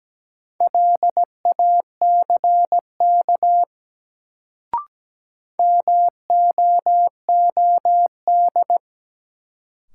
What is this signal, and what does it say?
Morse code, heard as a keyed tone, about 17 wpm, 700 Hz. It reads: LACK MOOD
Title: Sentences from Top 1100 Words - New Word Drill - 17wpm